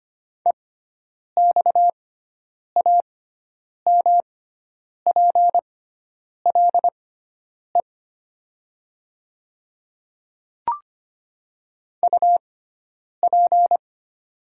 Morse code, heard as a keyed tone, about 25 wpm, spaced as for 9 wpm, 700 Hz. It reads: EXAMPLE UP
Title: 6x Character Spacing - Top 300 Words - 25wpm